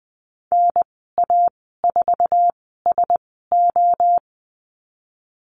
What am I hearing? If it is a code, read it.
Morse code, heard as a keyed tone, about 20 wpm, spaced as for 15 wpm, 700 Hz. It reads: NA4SO
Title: 048 Lesson 9g - Number 4 - Call Signs 20wpm